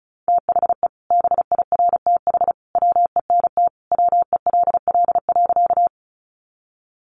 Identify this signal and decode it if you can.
Morse code, heard as a keyed tone, about 34 wpm, 700 Hz. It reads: THE BIRTH WENT WELL.
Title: Sentences from Top 800 Words 35wpm